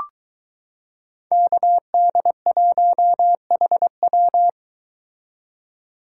Morse code, heard as a keyed tone, about 22 wpm, 700 Hz. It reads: KD1HW